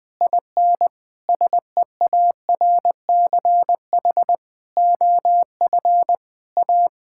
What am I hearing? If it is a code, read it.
Morse code, heard as a keyed tone, about 20 wpm, 700 Hz. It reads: IN SEARCH OF A
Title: Sets of 8 Words - 20wpm